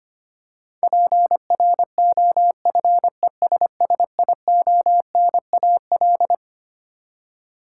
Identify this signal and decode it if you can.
Morse code, heard as a keyed tone, about 25 wpm, 700 Hz. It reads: PROFESSIONAL
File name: Twelve Letter Words - 25wpm